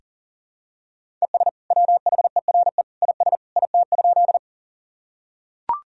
Morse code, heard as a keyed tone, about 37 wpm, 700 Hz. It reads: ES WHERE IS IT?
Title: Sentences from Top 200 Words and Abbreviations 40wpm